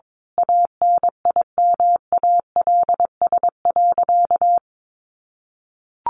Morse code, heard as a keyed tone, about 22 wpm, 700 Hz. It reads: ANIMALS.
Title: A Sound Of Thunder 22wpm